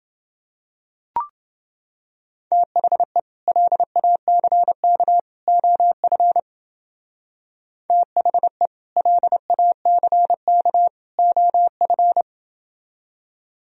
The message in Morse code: THE LACK OF THE LACK OF

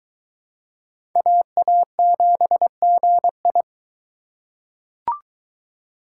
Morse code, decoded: AA7GI

